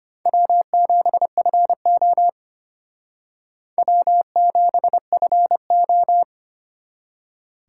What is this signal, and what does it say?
Morse code, heard as a keyed tone, about 32 wpm, 700 Hz. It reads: W7FO W7FO